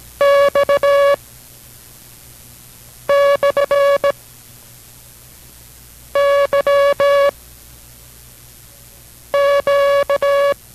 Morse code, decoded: X / Y Q

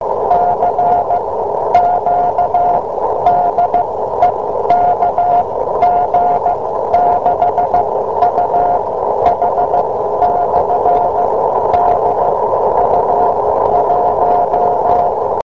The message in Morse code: CQ DE KG6UH/DU1

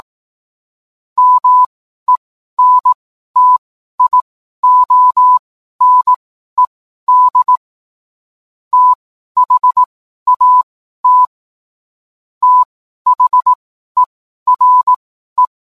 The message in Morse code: MENTIONED THAT THERE

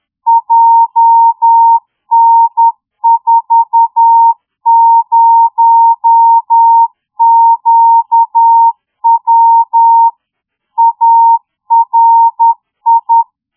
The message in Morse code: JN40QW ARI

